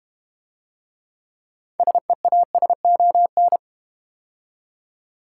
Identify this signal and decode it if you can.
Morse code, heard as a keyed tone, about 31 wpm, 700 Hz. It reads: SEASON
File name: Easy Words - Speed-Racing with Hints - 25wpm